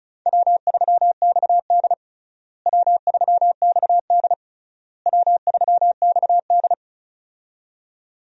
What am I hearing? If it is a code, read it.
Morse code, heard as a keyed tone, about 36 wpm, 700 Hz. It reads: W3XD W3XD W3XD
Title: USA Call Signs - 4 Characters - Repeated 3x - 35wpm